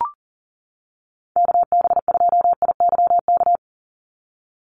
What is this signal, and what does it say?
Morse code, heard as a keyed tone, about 40 wpm, 700 Hz. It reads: KB2IYK
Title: USA Call Signs 40wpm